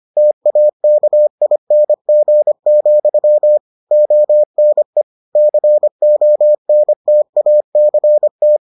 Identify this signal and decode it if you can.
Morse code, heard as a keyed tone, about 25 wpm, 600 Hz. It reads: TAKING, ONE CONTACT